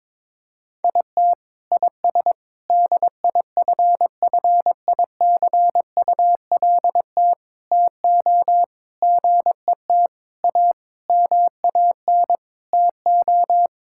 Morse code, heard as a keyed tone, about 22 wpm, 700 Hz. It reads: IT IS DIFFICULT TO GET A MAN TO
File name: Sets of 9 Words - 22wpm